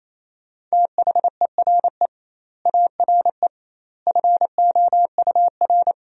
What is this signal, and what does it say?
Morse code, heard as a keyed tone, about 28 wpm, 700 Hz. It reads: THERE ARE FOUR
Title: 2x Word-Spacing - Sentences from Top 400 Words - 28wpm